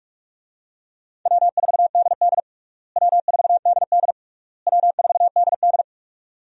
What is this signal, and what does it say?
Morse code, heard as a keyed tone, about 46 wpm, 700 Hz. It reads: W4DD W4DD W4DD
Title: USA Call Signs - 4 Characters - Repeated 3x - 45wpm